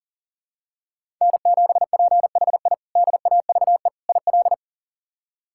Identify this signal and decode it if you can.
Morse code, heard as a keyed tone, about 40 wpm, 700 Hz. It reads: N7PHI DAVE IL
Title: K1USN SST QSO Practice - 40wpm